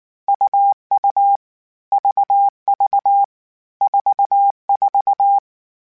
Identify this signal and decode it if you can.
Morse code, heard as a keyed tone, about 19 wpm, 800 Hz. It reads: UU VV 44